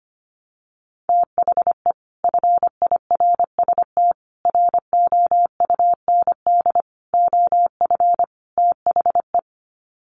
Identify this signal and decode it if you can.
Morse code, heard as a keyed tone, about 25 wpm, 700 Hz. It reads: THE FIRST ROUND OF THE
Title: Sets of 5 Words Encore 25wpm